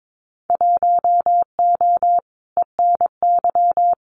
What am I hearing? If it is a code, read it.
Morse code, heard as a keyed tone, about 22 wpm, 700 Hz. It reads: 1O ENY
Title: Winter Field Day Exchanges 22wpm